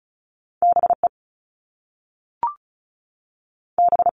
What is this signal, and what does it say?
Morse code, heard as a keyed tone, about 36 wpm, 690 Hz. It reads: BE B
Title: All Cut Numbers - 2-Digit 35wpm